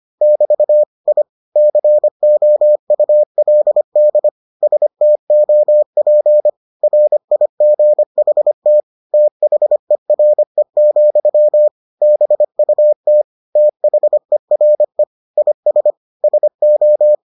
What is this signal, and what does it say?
Morse code, heard as a keyed tone, about 25 wpm, 600 Hz. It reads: <BT> I COULD STOP RIGHT THERE, BUT THERE IS SO